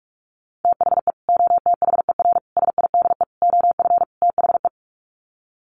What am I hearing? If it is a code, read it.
Morse code, heard as a keyed tone, about 44 wpm, 700 Hz. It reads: THE OTHER SIDE OF THE
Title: Sets of 5 Words 45wpm